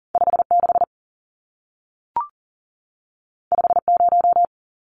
Morse code, heard as a keyed tone, about 40 wpm, 700 Hz. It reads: 56 50
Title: Numbers - 2-Digit 40wpm